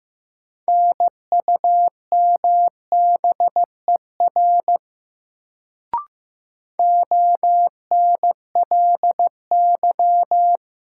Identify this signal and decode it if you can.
Morse code, heard as a keyed tone, about 15 wpm, 700 Hz. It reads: NUMBER ONLY
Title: Sets of 2 Words - New Word Drill - 15wpm